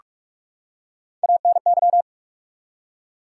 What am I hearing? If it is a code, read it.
Morse code, heard as a keyed tone, about 47 wpm, 700 Hz. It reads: ANY
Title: Three Letter Words - 45wpm